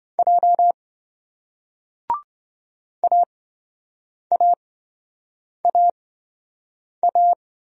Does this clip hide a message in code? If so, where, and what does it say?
Yes: Morse code from 0.2 to 7.3 s, heard as a keyed tone, about 26 wpm, spaced as for 7 wpm, 700 Hz. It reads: J AAAA